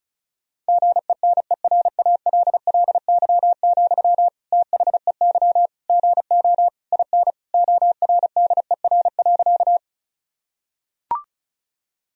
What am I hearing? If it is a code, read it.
Morse code, heard as a keyed tone, about 34 wpm, 700 Hz. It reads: GENERALLY, THEY GO IN ORDER.
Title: Sentences from Top 1000 Words - Review Part 4 - 35wpm